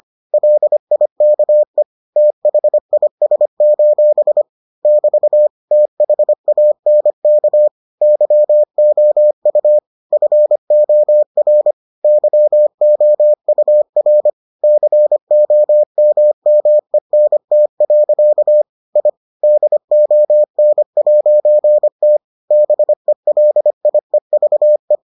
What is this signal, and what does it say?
Morse code, heard as a keyed tone, about 25 wpm, 600 Hz. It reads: LIKE THIS: <BT> THANK YOU FOR YOUR COMMENT. I DON'T BELIEVE